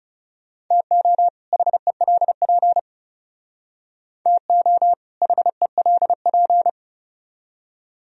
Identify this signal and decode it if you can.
Morse code, heard as a keyed tone, about 35 wpm, 700 Hz. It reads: TO HELP TO HELP
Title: Speed-Racing - Sets of 2 Words 30wpm